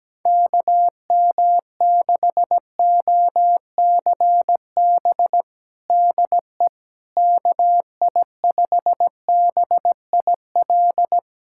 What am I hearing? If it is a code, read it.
Morse code, heard as a keyed tone, about 17 wpm, 700 Hz. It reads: KM6OCB DE KI5BIL
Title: USA Callsign DE Callsign - 6 Characters - 17wpm